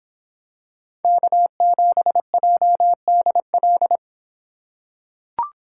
Morse code, heard as a keyed tone, about 24 wpm, 700 Hz. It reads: K7JDL